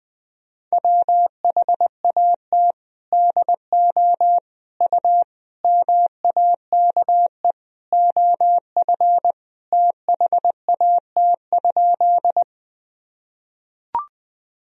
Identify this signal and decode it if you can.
Morse code, heard as a keyed tone, about 20 wpm, 700 Hz. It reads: WHAT DO U MAKE OF THAT?
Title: Sentences from Top 100 Words and Abbreviations 20wpm